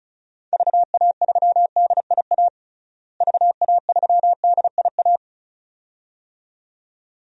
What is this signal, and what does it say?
Morse code, heard as a keyed tone, about 34 wpm, 700 Hz. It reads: VA3DIA VA3DIA
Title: DXCC-100 Call Signs - 6 Characters - Repeated 2x - 35wpm